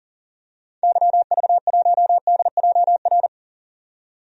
Morse code, heard as a keyed tone, about 37 wpm, 700 Hz. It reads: YV1DJR
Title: DXCC-100 Call Signs - 6 Characters - 40wpm